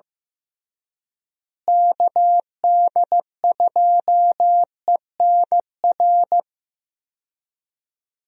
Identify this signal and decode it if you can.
Morse code, heard as a keyed tone, about 15 wpm, 700 Hz. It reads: KD2ENR
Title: USA Call Signs - 6 Characters - Repeated 2x - 15wpm